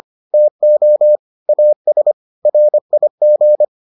Morse code, heard as a keyed tone, about 25 wpm, 600 Hz. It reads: TO AS RIG